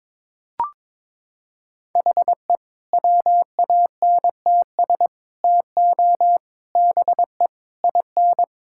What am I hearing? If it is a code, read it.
Morse code, heard as a keyed tone, about 22 wpm, 700 Hz. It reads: HE WANTS TO BE IN
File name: practice / 22wpm / Sentences with 4-Word Phrases - 22wpm